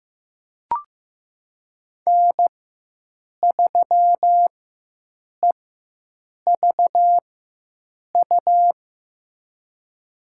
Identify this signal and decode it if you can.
Morse code, heard as a keyed tone, about 15 wpm, spaced as for 7 wpm, 700 Hz. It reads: N3EVU
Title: USA Call Signs - 4x Character Spacing - 15wpm